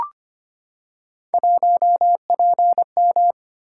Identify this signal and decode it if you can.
Morse code, heard as a keyed tone, about 25 wpm, 700 Hz. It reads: 1PM
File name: Three Letter-Number Combinations - 25wpm